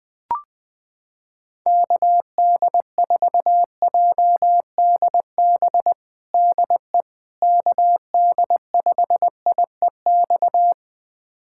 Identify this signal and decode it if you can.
Morse code, heard as a keyed tone, about 20 wpm, 700 Hz. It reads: KD4JDB DE KD5IEX
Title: USA Callsign DE Callsign 20wpm